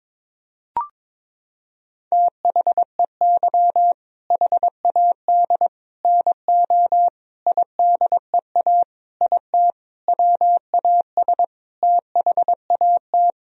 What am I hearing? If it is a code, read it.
Morse code, heard as a keyed tone, about 21 wpm, 700 Hz. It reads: THEY HAD NO IDEA IT WAS THAT
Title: Sentences from Top 1000 Words - Review Part 1 - 22wpm